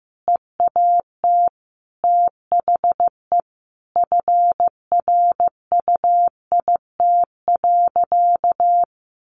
Morse code, heard as a keyed tone, about 15 wpm, 700 Hz. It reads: EAT THE FRUIT.
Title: Sentences from Top 1300 Words 15wpm